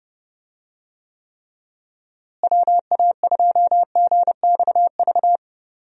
Morse code, heard as a keyed tone, about 31 wpm, 700 Hz. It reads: WA2GXV
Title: USA Call Signs - 6 Characters - Repeated 2x - 30wpm